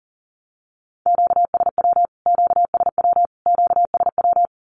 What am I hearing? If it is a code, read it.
Morse code, heard as a keyed tone, about 40 wpm, 700 Hz. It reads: QSW QSW QSW
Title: Q-Codes Repeated 3x 40wpm